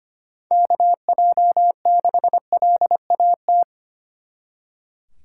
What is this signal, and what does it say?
Morse code, heard as a keyed tone, about 32 wpm, 700 Hz. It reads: KJ6LAT